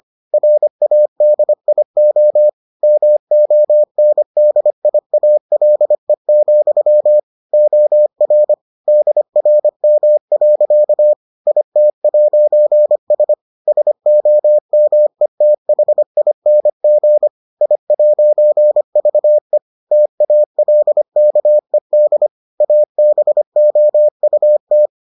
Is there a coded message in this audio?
Yes: Morse code from 0.3 to 24.9 s, heard as a keyed tone, about 25 wpm, 600 Hz. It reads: RADIO MONDIALE, OR DRM. IT'S SOMETHING I'VE TALKED ABOUT